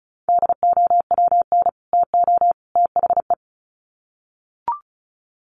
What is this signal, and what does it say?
Morse code, heard as a keyed tone, about 34 wpm, 700 Hz. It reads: DOWN TO THE
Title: Sets of 3 Words 35wpm